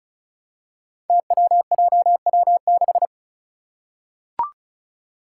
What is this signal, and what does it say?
Morse code, heard as a keyed tone, about 36 wpm, 700 Hz. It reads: TWJW6
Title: Five Challenging Letter-Number Combinations 35wpm